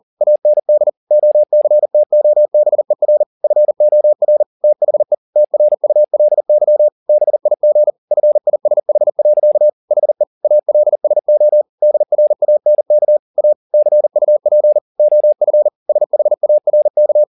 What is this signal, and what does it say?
Morse code, heard as a keyed tone, about 40 wpm, 600 Hz. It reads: AND OCTOBER FOR THE TRULY BIG FISH. HE ALSO DRANK A CUP OF SHARK